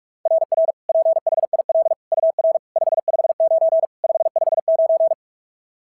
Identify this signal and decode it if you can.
Morse code, heard as a keyed tone, about 44 wpm, 690 Hz. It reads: RR PHIL UR 559 559
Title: SOTA QSO Practice 45wpm